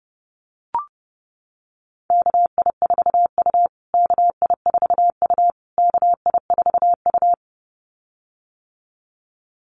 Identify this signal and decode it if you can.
Morse code, heard as a keyed tone, about 30 wpm, 700 Hz. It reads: KI4U KI4U KI4U
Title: USA Call Signs Repeated 3x 30wpm